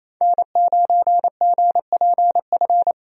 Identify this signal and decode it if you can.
Morse code, heard as a keyed tone, about 28 wpm, 700 Hz. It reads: N9GPF
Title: USA Call Signs - 5 Characters - 28wpm